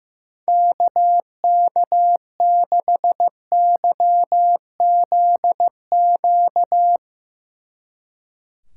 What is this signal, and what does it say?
Morse code, heard as a keyed tone, about 15 wpm, 700 Hz. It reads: KK6YZQ
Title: Speed Racing - USA 2x3 Call Signs 15wpm